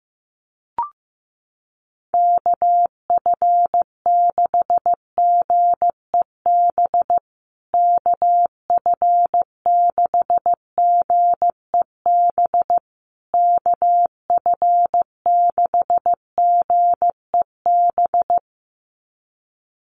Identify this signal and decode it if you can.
Morse code, heard as a keyed tone, about 15 wpm, 700 Hz. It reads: KF6GEB KF6GEB KF6GEB
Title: USA Call Signs Repeated 3x 15wpm